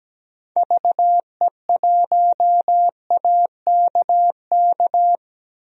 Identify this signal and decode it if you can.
Morse code, heard as a keyed tone, about 17 wpm, 700 Hz. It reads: VE1AKK
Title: DXCC-100 Call Signs - 6 Characters - 17wpm